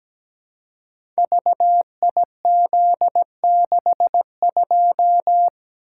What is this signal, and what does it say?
Morse code, heard as a keyed tone, about 17 wpm, 700 Hz. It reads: VIZ62
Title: Code Groups - 5 Letter:Numbers - 17wpm